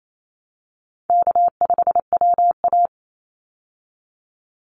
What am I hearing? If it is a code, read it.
Morse code, heard as a keyed tone, about 28 wpm, 700 Hz. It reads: K5WA
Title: Easy Call Signs Repeated 2x - 28wpm